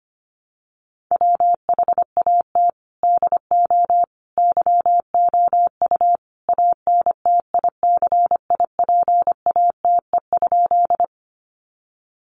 Morse code, heard as a keyed tone, about 25 wpm, 700 Hz. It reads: WHAT DO YOU ANTICIPATE?
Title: Sentences from Top 1500 Words 25wpm